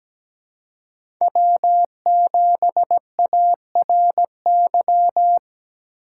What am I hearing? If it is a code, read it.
Morse code, heard as a keyed tone, about 17 wpm, 700 Hz. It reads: W7ARY
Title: USA Call Signs - 5 Characters - 3x Character Spacing - 17wpm